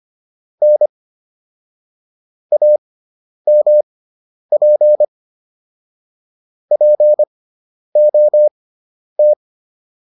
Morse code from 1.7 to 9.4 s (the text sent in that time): AMP POT